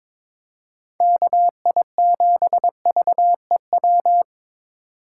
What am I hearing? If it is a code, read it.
Morse code, heard as a keyed tone, about 23 wpm, 700 Hz. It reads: KI7VEW
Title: USA Call Signs - 3x Character Spacing - 22wpm